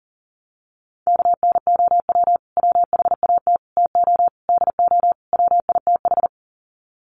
Morse code, heard as a keyed tone, about 40 wpm, 700 Hz. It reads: KNOW WHAT TO DO WITH
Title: Sets of 5 Words 40wpm